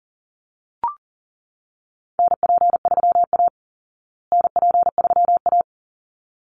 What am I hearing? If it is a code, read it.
Morse code, heard as a keyed tone, about 37 wpm, 700 Hz. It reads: NP3A NP3A
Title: Call Signs Repeated 3x - 40wpm